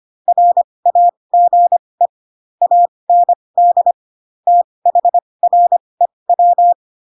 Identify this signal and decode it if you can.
Morse code, heard as a keyed tone, about 25 wpm, spaced as for 20 wpm, 700 Hz. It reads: RAGE AND THREW
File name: Pinocchio / Pinocchio ch2 15 25 ik1hge